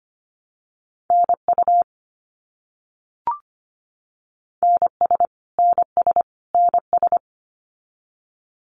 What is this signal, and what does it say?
Morse code, heard as a keyed tone, about 25 wpm, 700 Hz. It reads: NU NS NS NS
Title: Canadian Provinces and Territories - Repeated 3x - 25wpm